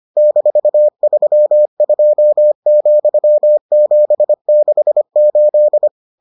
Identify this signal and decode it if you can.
Morse code, heard as a keyed tone, about 25 wpm, 600 Hz. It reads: -32,768